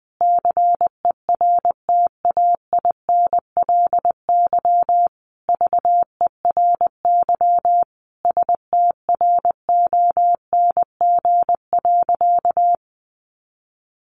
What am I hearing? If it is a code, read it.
Morse code, heard as a keyed tone, about 20 wpm, 700 Hz. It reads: CERTAINLY VERY STRONG.
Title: Sentences from Top 1300 Words 20wpm